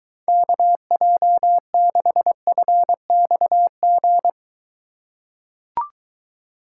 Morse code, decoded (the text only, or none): KJ6FXG